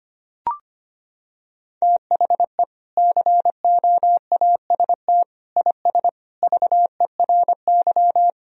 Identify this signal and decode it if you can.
Morse code, heard as a keyed tone, about 25 wpm, 700 Hz. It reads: THE COAST IS VERY
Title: Sentences from Top 1000 Words - Review Part 4 - 25wpm